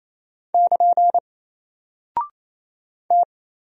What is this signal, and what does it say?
Morse code, heard as a keyed tone, about 27 wpm, 700 Hz. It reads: <KN> T
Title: Single Character (Letter:Number:Punctuation:Prosign) 28wpm